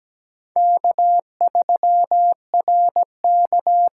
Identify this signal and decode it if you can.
Morse code, heard as a keyed tone, about 17 wpm, 700 Hz. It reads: K3RK
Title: USA Call Signs - 5 Characters - 8x Character Spacing - 17wpm